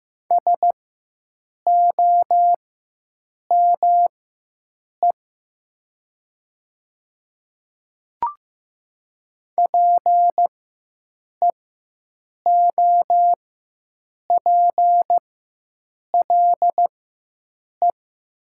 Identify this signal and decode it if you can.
Morse code, heard as a keyed tone, about 15 wpm, spaced as for 7 wpm, 700 Hz. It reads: SOME PEOPLE
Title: 4x Character Spacing - Top 200 Words - 15wpm